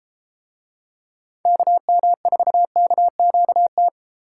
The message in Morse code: KM4KQT

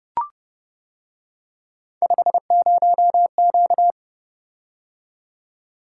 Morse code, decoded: 50Q

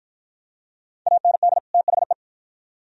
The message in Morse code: AND THE

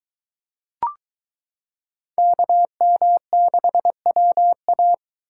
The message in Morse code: KM6WA

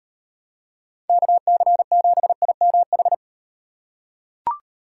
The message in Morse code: KC7IMH

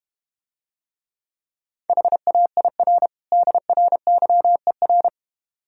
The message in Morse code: HAIR DRYER

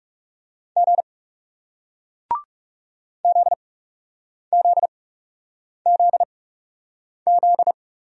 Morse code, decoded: G ZZZZ

